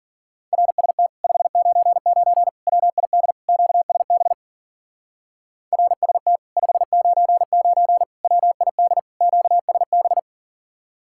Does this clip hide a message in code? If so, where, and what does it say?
Yes: Morse code from 0.5 to 10.2 s, heard as a keyed tone, about 52 wpm, 700 Hz. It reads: RST 599 WID QSB RST 599 WID QSB